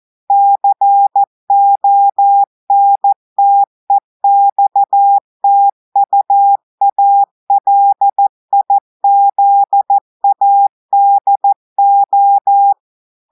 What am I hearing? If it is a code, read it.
Morse code, heard as a keyed tone, about 14 wpm, 800 Hz. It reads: CONTEXTUALIZADO
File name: audios / portugais / portugais texte en clair 1 vitesse 14 mots minutes